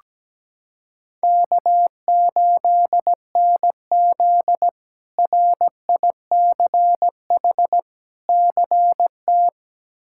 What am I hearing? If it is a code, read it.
Morse code, heard as a keyed tone, about 17 wpm, 700 Hz. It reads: K8NZ RICH CT
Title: K1USN SST QSO Practice - 17wpm